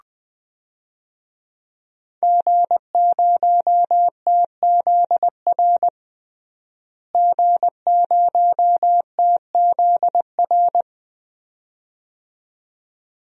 Morse code, heard as a keyed tone, about 20 wpm, 700 Hz. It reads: G0TZR G0TZR
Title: DXCC-100 Call Signs - 5 Characters - Repeated 2x - 20wpm